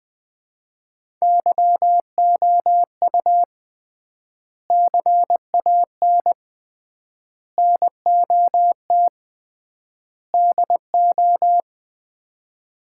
Morse code, heard as a keyed tone, about 21 wpm, 700 Hz. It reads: YOU CAN NOT DO
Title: 3x Word-Spacing - Sets of 5 Words - 20wpm